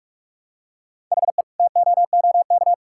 Morse code, heard as a keyed tone, about 46 wpm, 700 Hz. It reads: HE TOOK
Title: Sentences from Top 1600 Words 45wpm